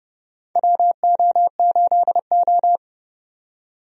Morse code, heard as a keyed tone, about 32 wpm, 700 Hz. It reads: WO8O
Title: USA Call Signs - 4 Characters - 30wpm